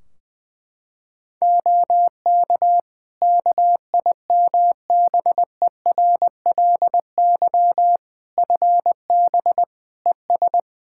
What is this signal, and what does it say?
Morse code, heard as a keyed tone, about 20 wpm, 700 Hz. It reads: OK KIMBERLY FB ES
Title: Practice QSO - Exchange 2 - Info - 20wpm